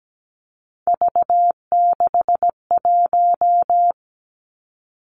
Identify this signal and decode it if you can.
Morse code, heard as a keyed tone, about 17 wpm, 700 Hz. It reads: V61
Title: Three Challenging Letter-Number Combinations 17wpm